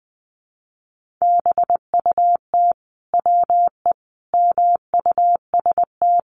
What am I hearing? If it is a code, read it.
Morse code, heard as a keyed tone, about 20 wpm, 700 Hz. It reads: BUT WE MUST
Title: Sentences from Top 1500 Words 20wpm